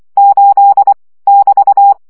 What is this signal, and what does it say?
Morse code, heard as a keyed tone, about 24 wpm, 800 Hz. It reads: 8 <BT>
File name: CW gyakorlas 8 20 baud